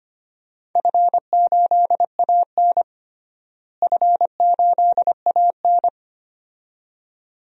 Morse code, heard as a keyed tone, about 25 wpm, 700 Hz. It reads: F8AN F8AN
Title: DXCC-100 Call Signs - 4 Characters - Repeated 2x - 25wpm